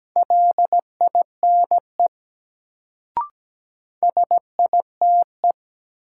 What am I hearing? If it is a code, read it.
Morse code, heard as a keyed tone, about 17 wpm, 700 Hz. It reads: LINE SITE
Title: Sentences from Top 1000 Words - Review Part 1 - New Word Drill - 17wpm